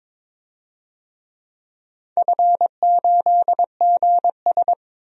Morse code, heard as a keyed tone, about 22 wpm, 700 Hz. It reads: F8GS
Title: DXCC-100 Call Signs - 4 Characters - 22wpm